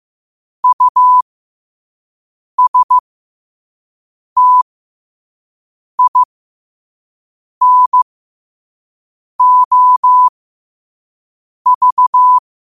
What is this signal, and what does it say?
Morse code, heard as a keyed tone, about 15 wpm, spaced as for 5 wpm, 1000 Hz. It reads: USTINOV
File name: qotdfiles / 5wpm